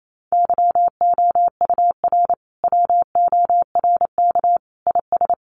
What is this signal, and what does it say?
Morse code, heard as a keyed tone, about 28 wpm, 700 Hz. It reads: YOUR WORK IS
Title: Sentences from Top 1100 Words 28wpm